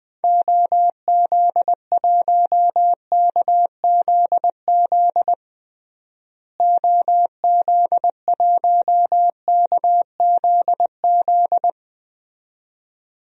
Morse code, heard as a keyed tone, about 20 wpm, 700 Hz. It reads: OZ1KZZ OZ1KZZ
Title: DXCC-100 Call Signs - 6 Characters - Repeated 2x - 20wpm